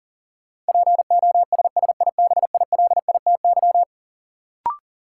Morse code, heard as a keyed tone, about 40 wpm, 700 Hz. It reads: POSSIBILITY
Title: Sets of 3 Words - New Word Drill - 40wpm